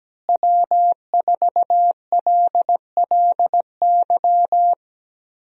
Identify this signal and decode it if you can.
Morse code, heard as a keyed tone, about 17 wpm, 700 Hz. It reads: W4LLY
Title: USA Call Signs - 5 Characters - Repeated 2x - 17wpm